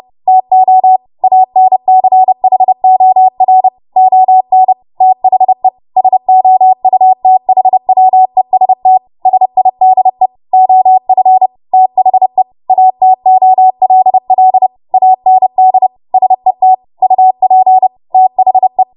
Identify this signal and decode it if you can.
Morse code, heard as a keyed tone, about 30 wpm, 750 Hz. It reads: TO ANCHOR ON THE SOUTHWEST SIDE OF THE ATOLL AND SET UP THE
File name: Morse / 30WPM